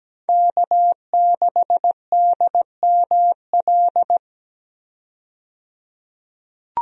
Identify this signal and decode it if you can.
Morse code, heard as a keyed tone, about 17 wpm, 700 Hz. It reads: K6DML
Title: USA Call Signs - 5 Characters - 17wpm